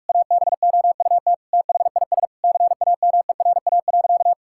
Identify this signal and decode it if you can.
Morse code, heard as a keyed tone, about 46 wpm, 690 Hz. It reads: ABOUT THIS CAMERA.
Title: Sentences from Top 1000 Words - Review Part 5 - 45wpm